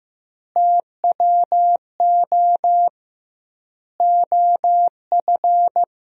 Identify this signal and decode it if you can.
Morse code, heard as a keyed tone, about 15 wpm, 700 Hz. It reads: TWO OF
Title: 2x Word-Spacing - Sentences from Top 100 Words - 15wpm